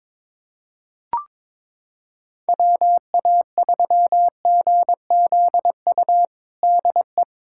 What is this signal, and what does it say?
Morse code, heard as a keyed tone, about 22 wpm, 700 Hz. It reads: WA3GZU DE
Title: USA Callsign DE Callsign - 6 Characters - 22wpm